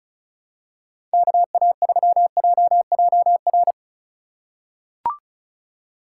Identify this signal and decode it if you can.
Morse code, heard as a keyed tone, about 36 wpm, 700 Hz. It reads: KA3JJR
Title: USA Call Signs - 6 Characters - 2x Character Spacing - 35wpm